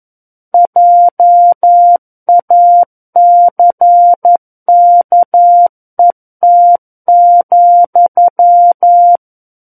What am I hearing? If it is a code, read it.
Morse code, heard as a keyed tone, about 11 wpm, 700 Hz. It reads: JACKET,
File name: 11 WPM